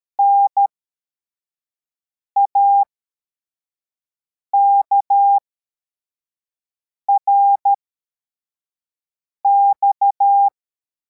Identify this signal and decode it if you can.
Morse code, heard as a keyed tone, about 13 wpm, spaced as for 4 wpm, 800 Hz. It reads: NAKRX
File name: formation / cw / Text21